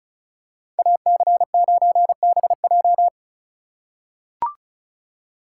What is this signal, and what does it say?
Morse code, heard as a keyed tone, about 34 wpm, 700 Hz. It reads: AC9BJ
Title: USA Call Signs - 2x Character Spacing - 35wpm